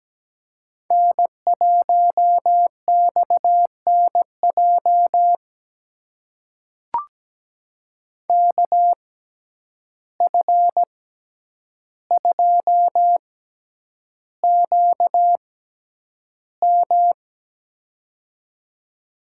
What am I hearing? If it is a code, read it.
Morse code, heard as a keyed tone, about 17 wpm, spaced as for 9 wpm, 700 Hz. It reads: N1XNJ KF2QM
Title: USA Call Signs - 5 Characters - 6x Character Spacing - 17wpm